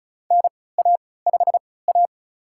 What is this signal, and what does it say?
Morse code, heard as a keyed tone, about 36 wpm, spaced as for 20 wpm, 700 Hz. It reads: NA5A
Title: USA Call Signs - 4 Characters - 3x Character Spacing - 35wpm